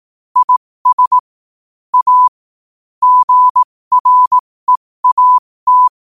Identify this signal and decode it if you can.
Morse code, heard as a keyed tone, about 18 wpm, spaced as for 15 wpm, 1000 Hz. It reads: IS A GREAT
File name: qotdfiles / 15wpm